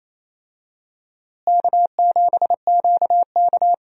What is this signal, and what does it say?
Morse code, heard as a keyed tone, about 28 wpm, 700 Hz. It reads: K7QK
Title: USA Call Signs - 4 Characters - Repeated 2x - 28wpm